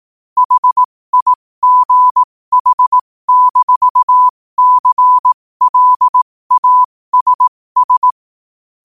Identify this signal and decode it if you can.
Morse code, heard as a keyed tone, about 18 wpm, spaced as for 15 wpm, 1000 Hz. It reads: HIGH-CLASS
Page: New quotes every day in morse code at 15 Words per minute.